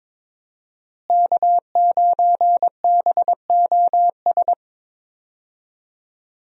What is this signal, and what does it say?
Morse code, heard as a keyed tone, about 22 wpm, 700 Hz. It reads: K9BOS
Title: USA Call Signs - 5 Characters - Repeated 3x - 22wpm